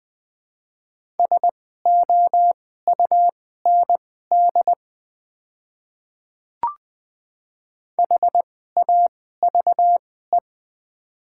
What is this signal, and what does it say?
Morse code, heard as a keyed tone, about 20 wpm, spaced as for 14 wpm, 700 Hz. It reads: SOUND HAVE
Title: 2x Character Spacing - Top 200 Words - 20wpm